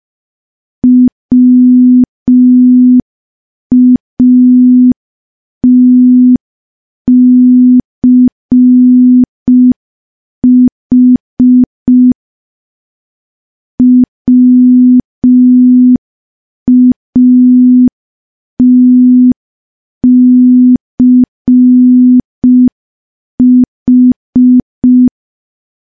Morse code, heard as a keyed tone, about 5 wpm, 250 Hz. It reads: WATCH WATCH
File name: morse / morse-training-05WPM